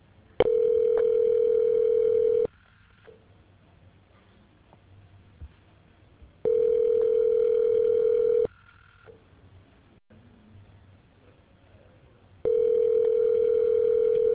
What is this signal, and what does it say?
Ringback tone (North America), 440 + 480 Hz.